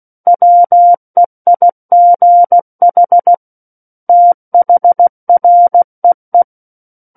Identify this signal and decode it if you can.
Morse code, heard as a keyed tone, about 16 wpm, 700 Hz. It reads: WEIGH THREE
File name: Book One / 07-Chapter 7 - 16 WPM